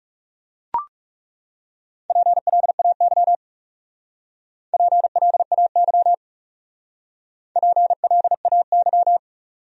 Morse code, heard as a keyed tone, about 38 wpm, 700 Hz. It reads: PLAY PLAY PLAY